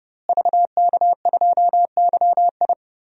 Morse code, heard as a keyed tone, about 30 wpm, 700 Hz. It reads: VK2YI
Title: DXCC-100 Call Signs - 5 Characters - Repeated 2x - 30wpm